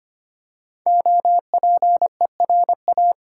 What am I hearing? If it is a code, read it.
Morse code, heard as a keyed tone, about 25 wpm, 700 Hz. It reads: OPERA
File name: 5 Letter Words - First 3 Letters Given - 25wpm